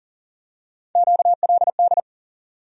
Morse code, heard as a keyed tone, about 40 wpm, 700 Hz. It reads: QLD
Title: USA Call Sign Postfixes - 3 Letters 40wpm